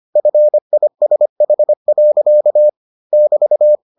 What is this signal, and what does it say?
Morse code, heard as a keyed tone, about 25 wpm, 600 Hz. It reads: FISH. <BT>